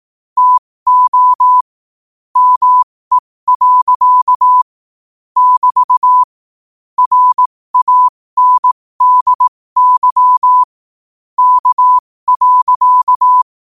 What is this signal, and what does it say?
Morse code, heard as a keyed tone, about 18 wpm, spaced as for 15 wpm, 1000 Hz. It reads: TO ME. <BT> RANDY K.
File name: qotdfiles / 15wpm